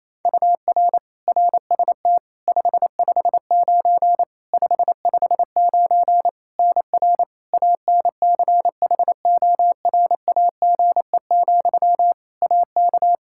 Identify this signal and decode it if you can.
Morse code, heard as a keyed tone, about 28 wpm, 700 Hz. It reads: UR RST 559 559 NR ANCHORAGE, AK
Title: Practice QSO - Exchange 1 - Beginning - 28wpm